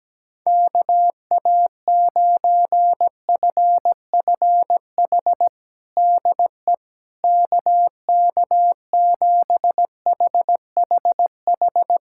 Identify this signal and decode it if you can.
Morse code, heard as a keyed tone, about 17 wpm, 700 Hz. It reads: KA9FFH DE KK7HHH
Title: USA Callsign DE Callsign - 6 Characters - 17wpm